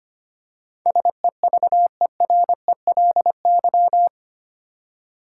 Morse code, heard as a keyed tone, about 25 wpm, 700 Hz. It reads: SEVERELY
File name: 8 Letter Words - First 3 Letters Given - 25wpm